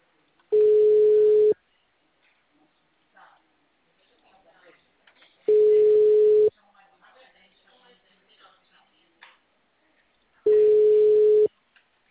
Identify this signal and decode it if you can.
Ringback tone (Europe, 425 Hz), with speech.